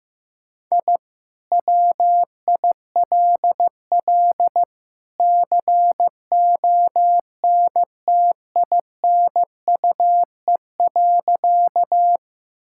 Morse code, heard as a keyed tone, about 16 wpm, 700 Hz. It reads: I WILL CONTINUE.
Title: Sentences from Top 1000 Words - Review Part 6 - 15wpm